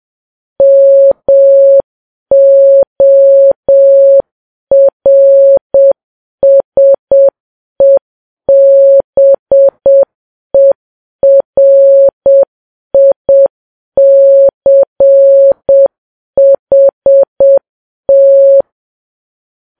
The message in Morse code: MORSEBERICHT